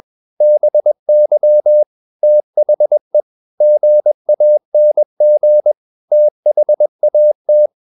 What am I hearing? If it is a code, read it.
Morse code, heard as a keyed tone, about 21 wpm, 600 Hz. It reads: BY THE GANG THAT
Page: CW med Gnister